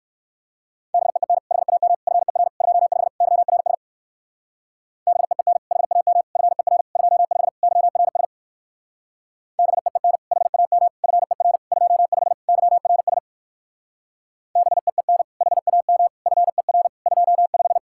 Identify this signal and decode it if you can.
Morse code, heard as a keyed tone, about 58 wpm, 700 Hz. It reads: BEEN HAM FER 25 YRS BEEN HAM FER 25 YRS BEEN HAM FER 25 YRS BEEN HAM FER 25